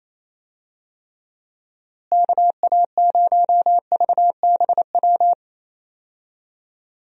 Morse code, heard as a keyed tone, about 28 wpm, 700 Hz. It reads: KA0VBW
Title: USA Call Signs - 6 Characters - Repeated 2x - 28wpm